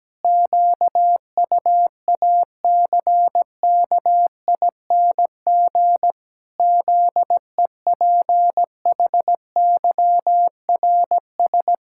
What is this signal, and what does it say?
Morse code, heard as a keyed tone, about 17 wpm, 700 Hz. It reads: QUACKING ZEPHYRS
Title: Pangram Challenge v2 - 17wpm